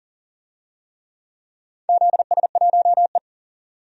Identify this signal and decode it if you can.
Morse code, heard as a keyed tone, about 37 wpm, 700 Hz. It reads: ZS1E
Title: DXCC-100 Call Signs - 4 Characters - 40wpm